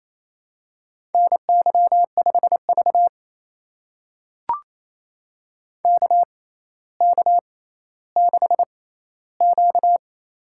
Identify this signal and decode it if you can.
Morse code, heard as a keyed tone, about 29 wpm, spaced as for 15 wpm, 700 Hz. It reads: NY5V KK6Q
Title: USA Call Signs - 4 Characters - 6x Character Spacing - 28wpm